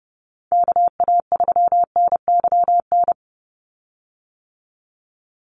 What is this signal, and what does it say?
Morse code, heard as a keyed tone, about 28 wpm, 700 Hz. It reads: KA3NYN
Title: USA Call Signs 30wpm